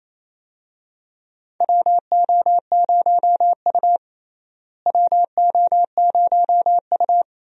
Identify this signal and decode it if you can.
Morse code, heard as a keyed tone, about 28 wpm, 700 Hz. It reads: WO0U WO0U
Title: USA Call Signs - 4 Characters - Repeated 3x - 28wpm